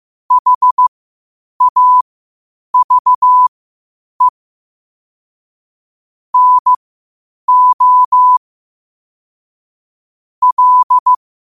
Morse code, heard as a keyed tone, about 15 wpm, spaced as for 8 wpm, 1000 Hz. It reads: HAVE NO L
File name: qotdfiles / 8wpm